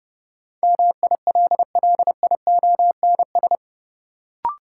Morse code, heard as a keyed tone, about 30 wpm, 700 Hz. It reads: MILLIONS
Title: Sets of 5 Words - New Word Drill - 30wpm